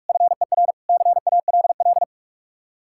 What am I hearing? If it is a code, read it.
Morse code, heard as a keyed tone, about 57 wpm, 700 Hz. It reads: FER CALL